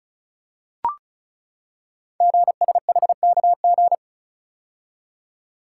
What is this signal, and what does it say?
Morse code, heard as a keyed tone, about 36 wpm, 700 Hz. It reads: GSHKG
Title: Code Groups - 5 Letters - 35wpm